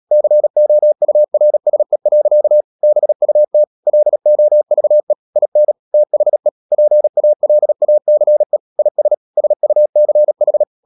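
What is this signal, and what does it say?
Morse code, heard as a keyed tone, about 37 wpm, 600 Hz. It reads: COURSE. BUT LOVE IN THE PALACE IS SUCH
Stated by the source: Morse code practice at 35–39 wpm sending English text of Ken Follett's novel Never, from CW med Gnister.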